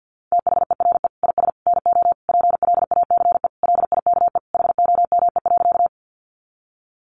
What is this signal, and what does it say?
Morse code, heard as a keyed tone, about 50 wpm, 700 Hz. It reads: THERE IS NO PLACE LIKE HOME.
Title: Sentences from Top 200 Words 50wpm